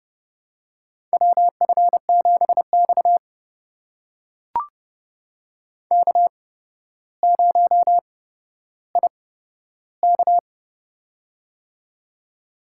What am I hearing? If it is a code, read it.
Morse code, heard as a keyed tone, about 30 wpm, spaced as for 13 wpm, 700 Hz. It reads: WF7X K0IK
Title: USA Call Signs - 4 Characters - 8x Character Spacing - 30wpm